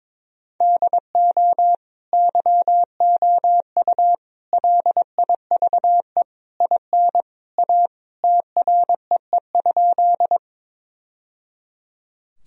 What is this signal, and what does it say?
Morse code, heard as a keyed tone, about 21 wpm, 700 Hz. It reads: DO YOU LIVE IN A TREE?
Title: Sentences from Top 1000 Words - Review Part 1 - 22wpm